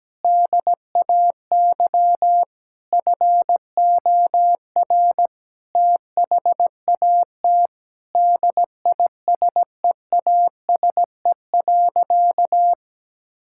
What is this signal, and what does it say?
Morse code, heard as a keyed tone, about 17 wpm, 700 Hz. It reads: DAY FOR THAT DISEASE.
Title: Sentences from Top 1000 Words - Review Part 6 - 17wpm